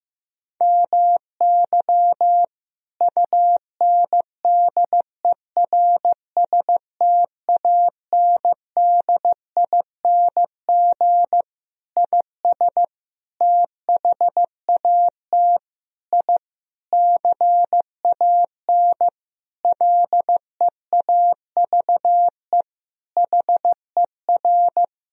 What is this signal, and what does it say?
Morse code, heard as a keyed tone, about 15 wpm, 700 Hz. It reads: MY UNDERSTANDING IS THAT I CAN LEAVE HER
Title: Sentences from Top 1000 Words - Review Part 3 - 15wpm